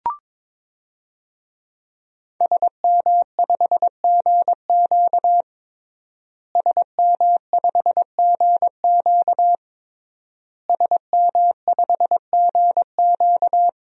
Morse code, heard as a keyed tone, about 22 wpm, 700 Hz. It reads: SM5GQ SM5GQ SM5GQ
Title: DXCC-100 Call Signs - 5 Characters - Repeated 3x - 22wpm